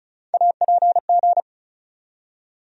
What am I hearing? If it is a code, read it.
Morse code, heard as a keyed tone, about 36 wpm, 700 Hz. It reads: APG
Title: USA Call Sign Postfixes - 3 Letters 35wpm